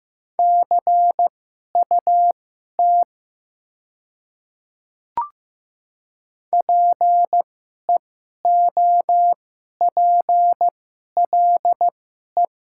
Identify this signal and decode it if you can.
Morse code, heard as a keyed tone, about 15 wpm, spaced as for 11 wpm, 700 Hz. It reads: CUT PEOPLE
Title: 2x Character Spacing - Top 300 Words - 15wpm